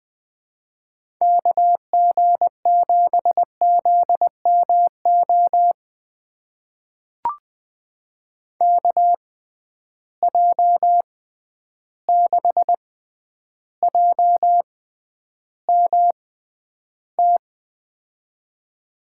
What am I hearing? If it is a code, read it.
Morse code, heard as a keyed tone, about 20 wpm, spaced as for 10 wpm, 700 Hz. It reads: KG7ZMO KJ6JMT
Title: USA Call Signs - 6 Characters - 6x Character Spacing - 20wpm